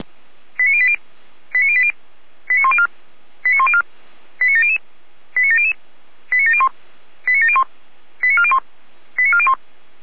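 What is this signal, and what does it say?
Five-tone selective call (ZVEI), heard as quick runs of five tones, tones of about 70 ms. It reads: 89080 89080 89104 89104 89700 89700 89701 89701 89401 89401
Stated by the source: ZVEI five-tone call received by SDR.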